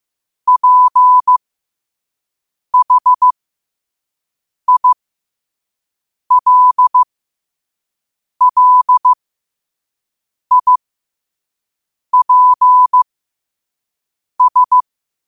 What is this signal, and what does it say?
Morse code, heard as a keyed tone, about 15 wpm, spaced as for 5 wpm, 1000 Hz. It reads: PHILLIPS
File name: qotdfiles / 5wpm